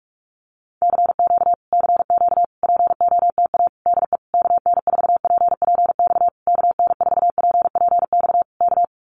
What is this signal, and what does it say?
Morse code, heard as a keyed tone, about 44 wpm, 700 Hz. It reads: CQ CQ POTA DE KN4PPX KN4PPX K
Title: Common QSO Phrases Encore 45wpm